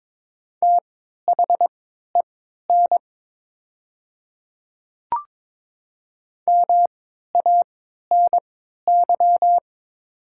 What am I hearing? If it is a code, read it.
Morse code, heard as a keyed tone, about 21 wpm, spaced as for 12 wpm, 700 Hz. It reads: THEN MANY
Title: 3x Character Spacing - Top 100 Words - 22wpm